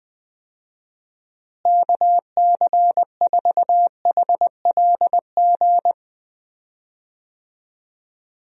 Morse code, heard as a keyed tone, about 21 wpm, 700 Hz. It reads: KC4HLG
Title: USA Call Signs - 6 Characters - Repeated 3x - 20wpm